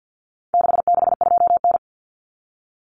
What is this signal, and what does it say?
Morse code, heard as a keyed tone, about 50 wpm, 700 Hz. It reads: 662N
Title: Common Cut Numbers - 4-Digit 50wpm